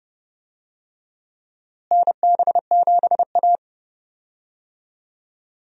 Morse code, heard as a keyed tone, about 29 wpm, 700 Hz. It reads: NB7A
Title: USA Call Signs - 4 Characters - Repeated 2x - 30wpm